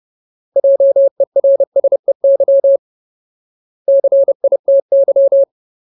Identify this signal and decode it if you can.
Morse code, heard as a keyed tone, about 30 wpm, 550 Hz. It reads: JERSEY CITY
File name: Ultimate Word List / 30WPM Ultimate